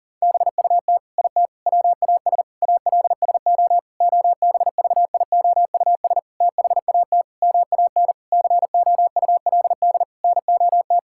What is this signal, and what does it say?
Morse code, heard as a keyed tone, about 40 wpm, 700 Hz. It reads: BUT IT WAS ALSO OBVIOUS THAT MAN COULD NOT
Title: Sentences from Top 1600 Words 40wpm